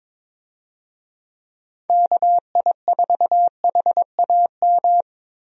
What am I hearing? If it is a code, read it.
Morse code, heard as a keyed tone, about 22 wpm, 700 Hz. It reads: KI4HAM
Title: USA Call Signs - 6 Characters - 22wpm